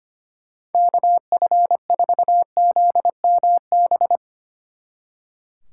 Morse code, heard as a keyed tone, about 24 wpm, 700 Hz. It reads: KF4ZMB
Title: Speed Racing - USA 2x3 Call Signs 25wpm